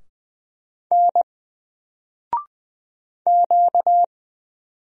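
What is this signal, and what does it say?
Morse code, heard as a keyed tone, about 20 wpm, 700 Hz. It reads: N Q
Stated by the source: Single letter and number Morse drill at 20 wpm, from N4LCD.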